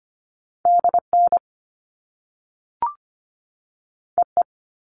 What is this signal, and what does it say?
Morse code, heard as a keyed tone, about 25 wpm, 700 Hz. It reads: DN EE
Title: All Cut Numbers - 2-Digit 25wpm